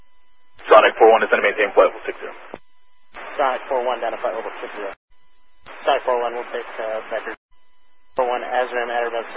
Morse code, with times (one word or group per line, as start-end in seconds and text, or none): none